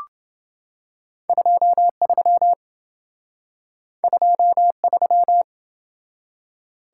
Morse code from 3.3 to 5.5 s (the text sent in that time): 23